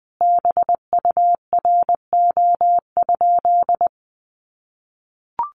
Morse code, heard as a keyed tone, about 20 wpm, 700 Hz. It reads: BURO?
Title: Common QSO Phrases 20wpm